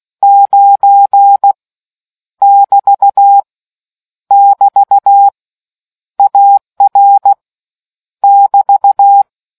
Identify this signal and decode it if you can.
Morse code, heard as a keyed tone, about 16 wpm, 800 Hz. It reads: 9 <BT> <BT> AR <BT>